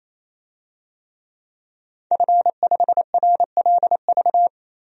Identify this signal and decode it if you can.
Morse code, heard as a keyed tone, about 28 wpm, 700 Hz. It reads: F5RLV
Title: DXCC-100 Call Signs - 5 Characters - 28wpm